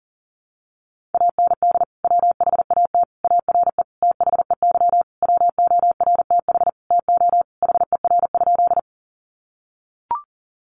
Morse code, heard as a keyed tone, about 40 wpm, 700 Hz. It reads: AND WHAT ARE THEY WORTH TO HER?
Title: Sentences from Top 800 Words 40wpm